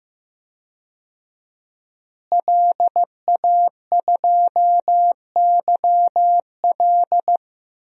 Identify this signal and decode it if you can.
Morse code, heard as a keyed tone, about 15 wpm, 700 Hz. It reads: LA2YL
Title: DXCC-100 Call Signs - 5 Characters - Repeated 2x - 15wpm